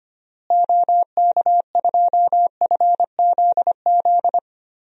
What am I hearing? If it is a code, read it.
Morse code, heard as a keyed tone, about 25 wpm, 700 Hz. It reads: OK2FZZ
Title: DXCC-100 Call Signs - 6 Characters - Repeated 3x - 25wpm